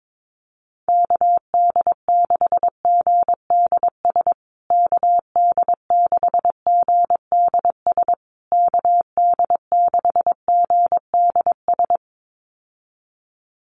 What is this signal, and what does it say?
Morse code, heard as a keyed tone, about 22 wpm, 700 Hz. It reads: KD6GDS KD6GDS KD6GDS
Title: USA Call Signs Repeated 3x 22wpm